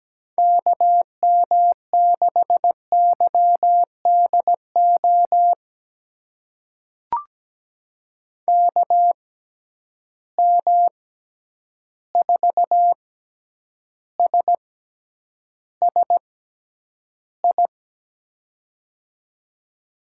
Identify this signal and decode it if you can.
Morse code, heard as a keyed tone, about 17 wpm, spaced as for 9 wpm, 700 Hz. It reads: KM6YDO KM4SSI
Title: USA Call Signs - 6 Characters - 6x Character Spacing - 17wpm